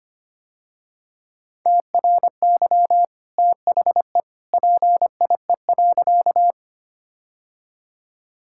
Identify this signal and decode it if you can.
Morse code, heard as a keyed tone, about 25 wpm, 690 Hz. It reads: TRY THE PIE.
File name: Easy Sentences with Hints - 25wpm